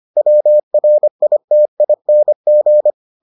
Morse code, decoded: WRITING

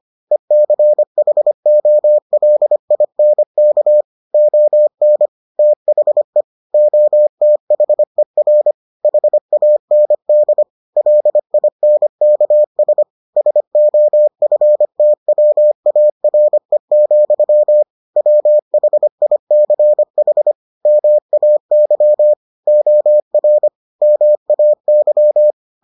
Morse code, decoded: ECHOLINK ON THE OTHER HAND LINKS SOFTWARE, WHICH MAY OR MAY

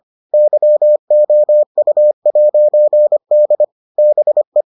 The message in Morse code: YOU'D BE

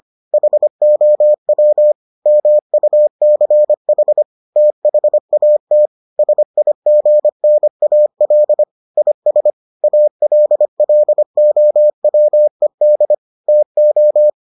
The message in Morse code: HOW MUCH THAT SIGNAL IS ALLOWED TO

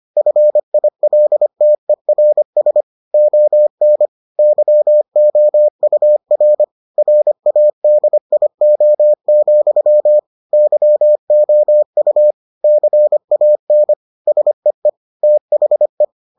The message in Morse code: FILTERS ON YOUR RADIO, YOU CAN SEE THE

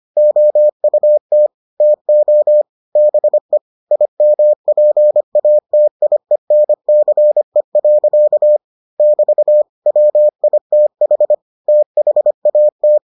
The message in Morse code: OUT TO BE IMPATIENCE. <BT> WITH THAT

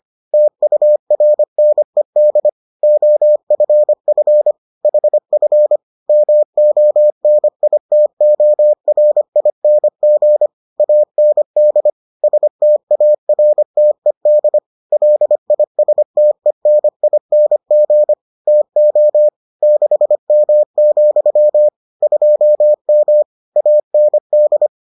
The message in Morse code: TURNED OFF HF MONITORING AND STARTED LISTENING TO 6M, 2M AND

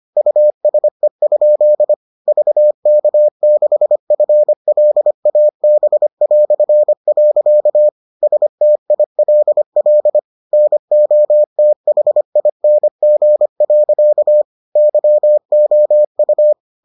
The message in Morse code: USE? VK6FLAB". STILL NOTHING. YOU

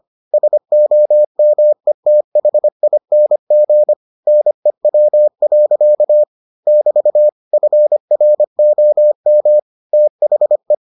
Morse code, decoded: SOMETHING NEW. <BT> FROM THE

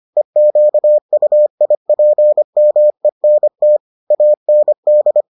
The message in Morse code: EQUIPMENT AND

